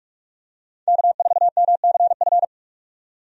Morse code, heard as a keyed tone, about 46 wpm, 700 Hz. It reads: K4MCF
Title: USA Call Signs - 5 Characters - 45wpm